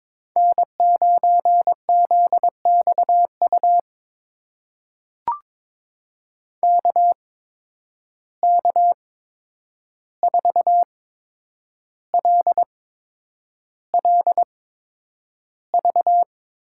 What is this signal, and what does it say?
Morse code, heard as a keyed tone, about 22 wpm, spaced as for 9 wpm, 700 Hz. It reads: N9ZXU KK4LLV
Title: USA Call Signs - 8x Character Spacing - 22wpm